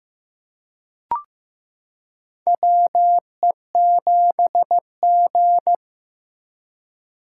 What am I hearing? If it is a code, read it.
Morse code, heard as a keyed tone, about 16 wpm, 700 Hz. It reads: WE7G
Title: K1USN SST QSO Practice - 15wpm